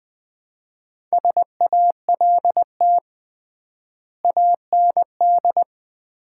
Morse code, heard as a keyed tone, about 19 wpm, 700 Hz. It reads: SALT AND
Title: 3x Word-Spacing - Sets of 5 Words - 20wpm